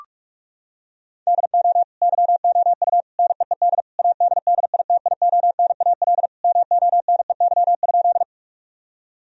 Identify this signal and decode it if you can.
Morse code, heard as a keyed tone, about 41 wpm, 700 Hz. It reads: DO YOU NEED ADDITIONAL MONEY?
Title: Sentences from Top 1000 Words - Review Part 4 - 45wpm